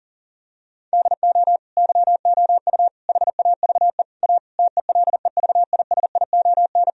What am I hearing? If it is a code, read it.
Morse code, heard as a keyed tone, about 37 wpm, 700 Hz. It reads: DO YOU HAVE A TELEVISION
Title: Sentences from Top 1000 Words - Review Part 4 - 40wpm